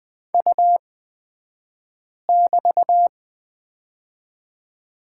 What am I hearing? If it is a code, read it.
Morse code, heard as a keyed tone, about 20 wpm, spaced as for 5 wpm, 690 Hz. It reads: U<BT>
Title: 228 Lesson 37d - Letter x - Two Characters 20wpm